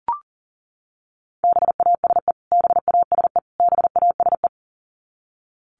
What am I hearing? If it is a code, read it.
Morse code, heard as a keyed tone, about 40 wpm, 700 Hz. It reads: BASE BASE BASE
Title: Top 300 Words Repeated 40wpm